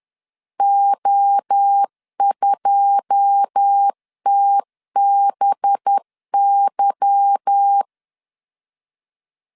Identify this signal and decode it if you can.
Morse code, heard as a keyed tone, about 10 wpm, 800 Hz. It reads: O2TBY
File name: LEZIONI / 9 2 10wpm 2736